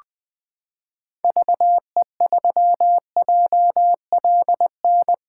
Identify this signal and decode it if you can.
Morse code, heard as a keyed tone, about 20 wpm, 700 Hz. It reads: VE3JLN
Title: 2023 POTA Activator Call Signs - 20wpm